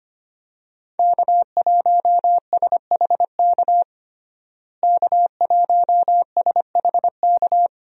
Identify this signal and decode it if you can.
Morse code, heard as a keyed tone, about 25 wpm, 700 Hz. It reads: K1SHK K1SHK
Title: USA Call Signs - 5 Characters - Repeated 2x - 25wpm